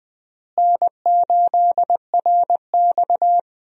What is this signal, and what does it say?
Morse code, heard as a keyed tone, about 20 wpm, 700 Hz. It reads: N8RX
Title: USA Call Signs - 4 Characters - Repeated 3x - 20wpm